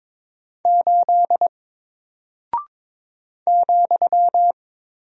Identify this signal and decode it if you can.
Morse code, heard as a keyed tone, about 22 wpm, 700 Hz. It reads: 8 ,
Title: Single Character (Letter:Number:Punctuation:Prosign) Alt 22wpm